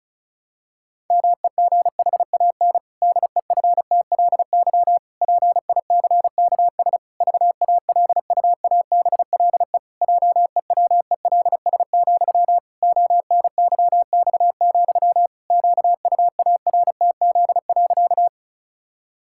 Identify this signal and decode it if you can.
Morse code, heard as a keyed tone, about 36 wpm, 700 Hz. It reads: MEGHAN DEFTLY PICKS VALUABLE JEWELS, ONYX, QUARTZ.
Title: Pangram Challenge v2 - 35wpm